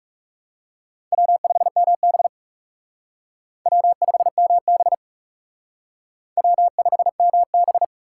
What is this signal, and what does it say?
Morse code, heard as a keyed tone, about 39 wpm, 700 Hz. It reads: W5MB W5MB W5MB